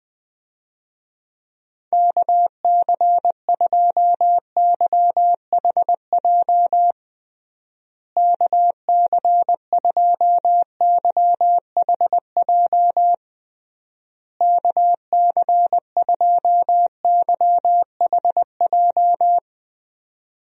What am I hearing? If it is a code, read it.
Morse code, heard as a keyed tone, about 21 wpm, 700 Hz. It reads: KC2YHJ KC2YHJ KC2YHJ
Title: USA Call Signs - 6 Characters - Repeated 3x - 20wpm